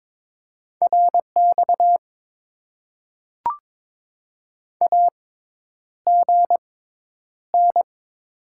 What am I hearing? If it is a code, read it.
Morse code, heard as a keyed tone, about 22 wpm, spaced as for 10 wpm, 700 Hz. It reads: RX AGN
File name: Top 90 QSO Elements - 6x Character Spacing - 22wpm